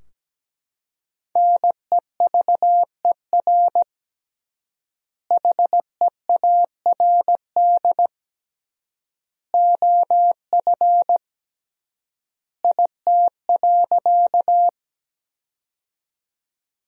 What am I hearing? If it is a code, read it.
Morse code, heard as a keyed tone, about 17 wpm, 700 Hz. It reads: NEVER HEARD OF IT.
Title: 3x Word-Spacing - Sentences from Top 400 Words - 17wpm